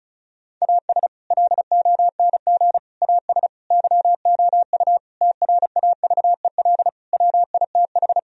Morse code, heard as a keyed tone, about 34 wpm, 700 Hz. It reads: AS LONG AS YOU TRAVEL WITH
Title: Sentences from Top 1000 Words - Review Part 2 - 35wpm